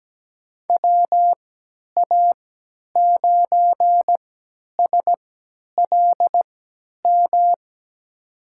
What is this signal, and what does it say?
Morse code, heard as a keyed tone, about 17 wpm, spaced as for 10 wpm, 700 Hz. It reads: WA9SLM
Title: USA Call Signs - 3x Character Spacing - 17wpm